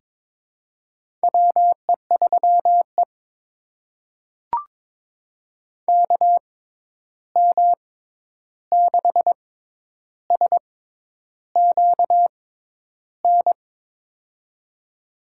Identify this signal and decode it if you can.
Morse code, heard as a keyed tone, about 22 wpm, spaced as for 10 wpm, 700 Hz. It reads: WE3E KM6SQN
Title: USA Call Signs - 6x Character Spacing - 22wpm